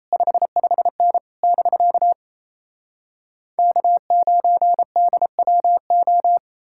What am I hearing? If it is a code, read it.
Morse code, heard as a keyed tone, about 37 wpm, 700 Hz. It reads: 55N <BK> K9DWO